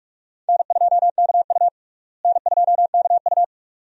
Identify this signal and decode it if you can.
Morse code, heard as a keyed tone, about 43 wpm, 700 Hz. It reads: N2KU N2KU
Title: USA Call Signs - 4 Characters - Repeated 2x - 45wpm